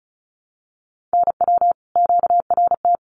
Morse code, heard as a keyed tone, about 34 wpm, 700 Hz. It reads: NW QRT
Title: Common QSO Phrases 35wpm